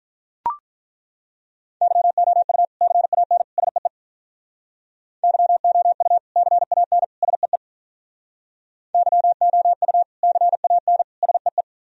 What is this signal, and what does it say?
Morse code, heard as a keyed tone, about 43 wpm, 700 Hz. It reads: YOU CAN SEE YOU CAN SEE YOU CAN SEE
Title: Speed-Racing - Sets of 3 Words 35wpm